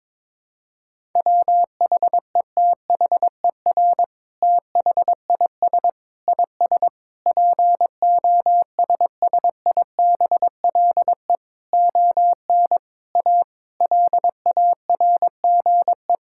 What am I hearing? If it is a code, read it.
Morse code, heard as a keyed tone, about 22 wpm, 700 Hz. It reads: WHETHER THIS IS POSSIBLE ON A LARGE
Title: Sentences from Top 1000 Words - Review Part 5 - 22wpm